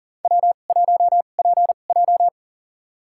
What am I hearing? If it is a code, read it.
Morse code, heard as a keyed tone, about 38 wpm, spaced as for 28 wpm, 700 Hz. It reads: W1PJ
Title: USA Call Signs - 4 Characters - 2x Character Spacing - 40wpm